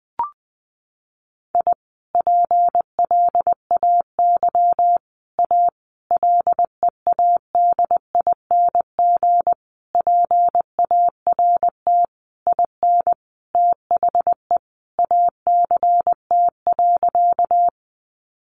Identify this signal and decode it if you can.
Morse code, heard as a keyed tone, about 20 wpm, 700 Hz. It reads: I PLAY A LEADING PART IN THE ACT.
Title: Sentences from Top 1100 Words 20wpm